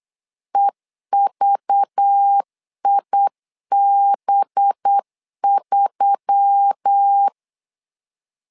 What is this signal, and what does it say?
Morse code, heard as a keyed tone, about 8 wpm, 800 Hz. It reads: EVIB3
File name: LEZIONI / 5 LBVC / 5 LBV 8wpm 2837